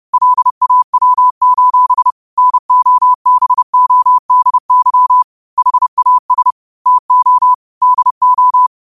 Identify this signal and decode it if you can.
Morse code, heard as a keyed tone, about 30 wpm, 1000 Hz. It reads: LAW: NOBODY HAS TO DO
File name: qotdfiles / 30wpm